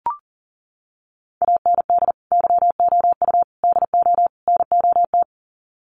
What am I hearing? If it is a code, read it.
Morse code, heard as a keyed tone, about 40 wpm, 700 Hz. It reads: AND YOU DO NOT
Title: Sets of 4 Words 40wpm